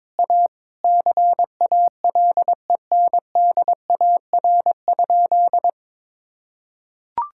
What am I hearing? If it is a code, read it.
Morse code, heard as a keyed tone, about 22 wpm, 700 Hz. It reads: A CALENDAR?
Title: Sentences from Top 1600 Words 22wpm